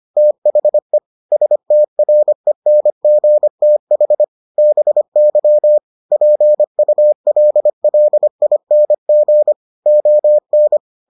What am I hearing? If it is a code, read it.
Morse code, heard as a keyed tone, about 25 wpm, 600 Hz. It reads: THE STRENGTH BY PULLING ON